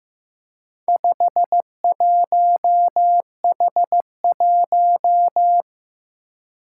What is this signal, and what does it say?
Morse code, heard as a keyed tone, about 15 wpm, 700 Hz. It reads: 51H1
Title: Four Challenging Letter-Number Combinations 15wpm